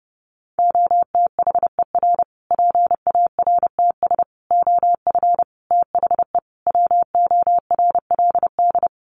Morse code, heard as a keyed tone, about 30 wpm, 700 Hz. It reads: OTHER PARTS OF THE WORLD
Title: Sets of 5 Words Encore 30wpm